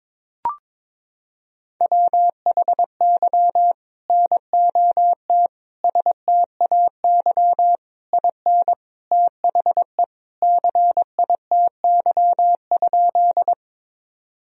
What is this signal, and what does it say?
Morse code, heard as a keyed tone, about 21 wpm, 700 Hz. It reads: WHY NOT STAY IN THE CITY?
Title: Sentences from Top 1000 Words - Review Part 1 - 22wpm